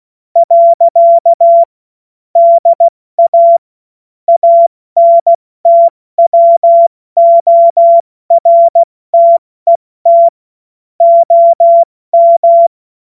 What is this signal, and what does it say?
Morse code, heard as a keyed tone, about 16 wpm, spaced as for 14 wpm, 680 Hz. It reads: . DA ANTWORTET OM
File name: Morsekurs / Witze / Witz 28. September 2021, 70 F80